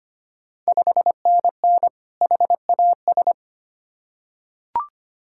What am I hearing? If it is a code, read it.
Morse code, heard as a keyed tone, about 25 wpm, 690 Hz. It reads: 5NN HAS
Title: Ontario QSO Party Exchange 25wpm